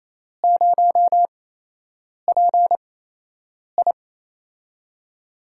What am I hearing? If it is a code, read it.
Morse code, heard as a keyed tone, about 28 wpm, spaced as for 8 wpm, 700 Hz. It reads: 0PI
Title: USA Call Signs - 5 Characters - 8x Character Spacing - 28wpm